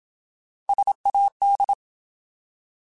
Morse code, heard as a keyed tone, about 27 wpm, 800 Hz. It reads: SAD
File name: Alphabetical / Alphabetical mp3s / Ss @25